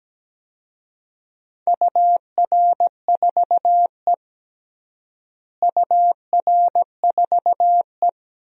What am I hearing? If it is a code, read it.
Morse code, heard as a keyed tone, about 17 wpm, 700 Hz. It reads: UR4E UR4E
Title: DXCC-100 Call Signs - 4 Characters - Repeated 2x - 17wpm